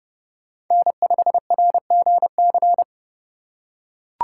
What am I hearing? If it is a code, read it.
Morse code, heard as a keyed tone, about 30 wpm, 700 Hz. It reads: N5RGC
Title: USA Call Signs - 5 Characters - 4x Character Spacing - 30wpm